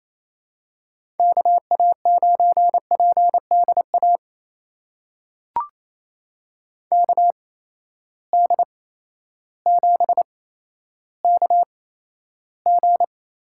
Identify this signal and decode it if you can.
Morse code, heard as a keyed tone, about 28 wpm, spaced as for 13 wpm, 700 Hz. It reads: KA9PDA KD7KG
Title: USA Call Signs - 6 Characters - 8x Character Spacing - 28wpm